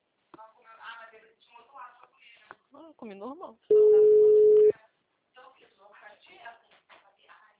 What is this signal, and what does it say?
Ringback tone (Europe, 425 Hz), with speech.